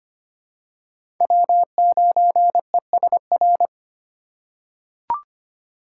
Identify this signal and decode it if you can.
Morse code, heard as a keyed tone, about 25 wpm, 700 Hz. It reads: W9ESR
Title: USA Call Signs - 5 Characters - 3x Character Spacing - 25wpm